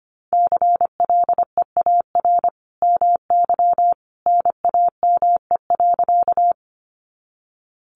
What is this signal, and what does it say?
Morse code, heard as a keyed tone, about 24 wpm, 700 Hz. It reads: CLEAR MY NAME.
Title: Sentences from Top 500 Words 25wpm